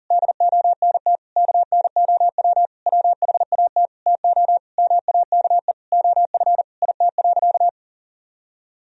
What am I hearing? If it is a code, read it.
Morse code, heard as a keyed tone, about 40 wpm, 700 Hz. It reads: DONT KNOW WHAT TO MAKE OF IT.
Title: Sentences from Top 1000 Words - Review Part 1 - 40wpm